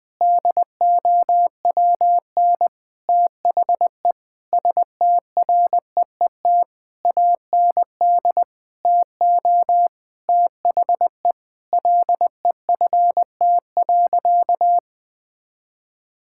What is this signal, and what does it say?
Morse code, heard as a keyed tone, about 20 wpm, 700 Hz. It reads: DOWN THE STREET AND TO THE LEFT.
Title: Sentences from Top 1600 Words 20wpm